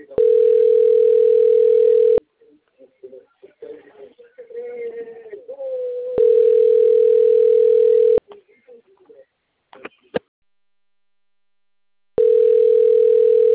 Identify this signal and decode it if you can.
Ringback tone (North America), 440 + 480 Hz, with speech.
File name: rg-1600-unknown- - - .97431